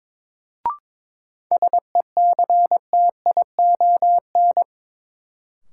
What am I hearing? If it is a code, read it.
Morse code, heard as a keyed tone, about 23 wpm, 700 Hz. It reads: SECTION
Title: Sentences from Top 1000 Words - Review Part 5 - New Word Drill - 22wpm